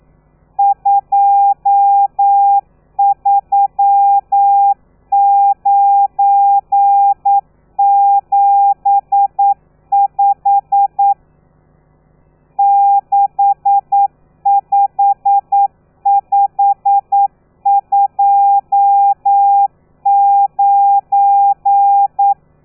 Morse code, heard as a keyed tone, about 9 wpm, 800 Hz. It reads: 23975 65529
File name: telegrafia / i2tfj / Lez12 40